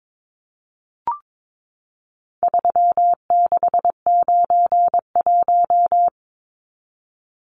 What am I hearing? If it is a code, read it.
Morse code, heard as a keyed tone, about 22 wpm, 700 Hz. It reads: 3691
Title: Numbers - 4-Digit 22wpm